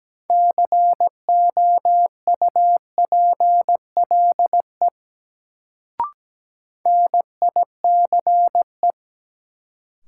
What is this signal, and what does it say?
Morse code, heard as a keyed tone, about 17 wpm, 700 Hz. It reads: COUPLE NICE
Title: Sentences from Top 1100 Words - New Word Drill - 17wpm